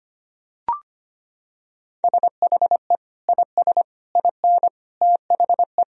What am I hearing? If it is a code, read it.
Morse code, heard as a keyed tone, about 25 wpm, 700 Hz. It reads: SHE IS IN THE
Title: Sentences from Top 1000 Words - Review Part 4 - 25wpm